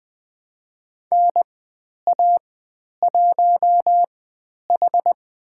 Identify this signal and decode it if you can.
Morse code, heard as a keyed tone, about 20 wpm, spaced as for 10 wpm, 700 Hz. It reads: NA1H
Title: 102 Lesson 17g - Letter u - Call Signs 20wpm